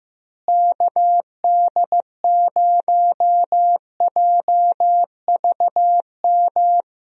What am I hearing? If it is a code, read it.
Morse code, heard as a keyed tone, about 15 wpm, 700 Hz. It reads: KD0JVM
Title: USA Call Signs - 6 Characters - Repeated 2x - 15wpm